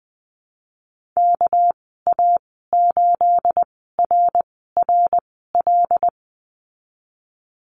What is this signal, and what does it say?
Morse code, heard as a keyed tone, about 19 wpm, spaced as for 14 wpm, 700 Hz. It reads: KA8RRL
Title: 213 Lesson 34g - Number 8 - Call Signs 20wpm